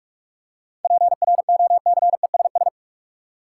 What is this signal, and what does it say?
Morse code, heard as a keyed tone, about 46 wpm, 700 Hz. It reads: PROCESS
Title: Seven Letter Words - 45wpm